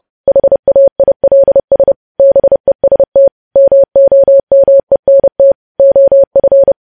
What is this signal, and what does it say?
Morse code, heard as a keyed tone, about 30 wpm, 550 Hz. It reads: HAILS BEST MOMENT OF
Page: news headlines in morse code at 30 wpm